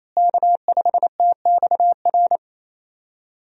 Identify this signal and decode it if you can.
Morse code, heard as a keyed tone, about 28 wpm, 700 Hz. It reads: K5TXR
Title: USA Call Signs - 5 Characters - Repeated 2x - 28wpm